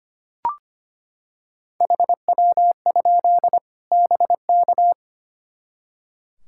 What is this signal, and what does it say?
Morse code, heard as a keyed tone, about 25 wpm, 700 Hz. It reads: HW? BK
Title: Practice QSO - Exchange 1 - Beginning - 25wpm